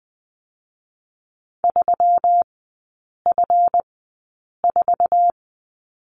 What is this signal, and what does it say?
Morse code, heard as a keyed tone, about 20 wpm, spaced as for 8 wpm, 700 Hz. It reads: 3F4
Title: 184 Lesson 30e - Letter b - Three Characters 20wpm